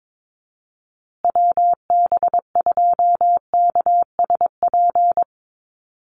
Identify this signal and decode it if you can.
Morse code, heard as a keyed tone, about 22 wpm, 700 Hz. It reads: WB2KSP
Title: USA Call Signs Repeated 3x 22wpm